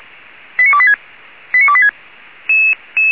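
Five-tone selective call (ZVEI), heard as quick runs of five tones, tones of about 70 ms. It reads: 89286 89286
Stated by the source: ZVEI five-tone call received by SDR, from feuerwehr-orlamuende.org (fire brigade Orlamünde).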